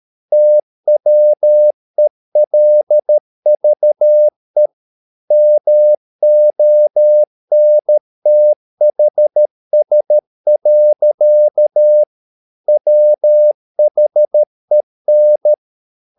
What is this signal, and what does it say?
Morse code, heard as a keyed tone, about 13 wpm, 600 Hz. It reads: TWELVE MONTHS. WHEN
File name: mp3 files / Lexiphiles - 13wpm